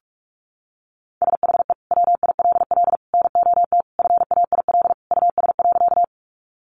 Morse code, heard as a keyed tone, about 44 wpm, 700 Hz. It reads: SHE WILL NOT FAIL US.
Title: Sentences from Top 1100 Words 45wpm